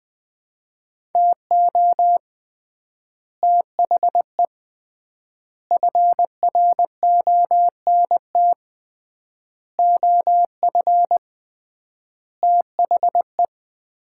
Morse code, heard as a keyed tone, about 20 wpm, 700 Hz. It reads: TO THE FRONT OF THE
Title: 3x Word-Spacing - Sets of 5 Words - 20wpm